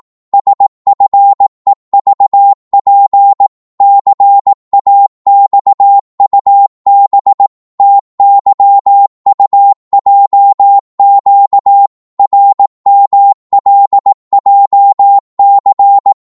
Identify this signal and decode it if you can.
Morse code, heard as a keyed tone, about 18 wpm, 800 Hz. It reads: SFEVP CAXUB TYUJQ RMLJC